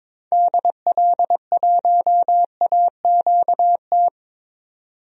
Morse code, heard as a keyed tone, about 22 wpm, 700 Hz. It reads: DL1AQT
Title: DXCC-100 Call Signs - 6 Characters - 22wpm